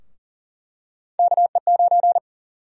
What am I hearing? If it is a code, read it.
Morse code, heard as a keyed tone, about 40 wpm, 700 Hz. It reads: KE9
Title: USA Call Sign Prefixes - 40wpm